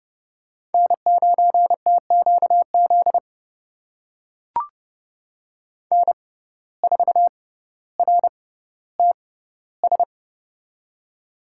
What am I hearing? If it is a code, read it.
Morse code, heard as a keyed tone, about 30 wpm, spaced as for 15 wpm, 700 Hz. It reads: N9TQZ N4RTS
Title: USA Call Signs - 5 Characters - 6x Character Spacing - 30wpm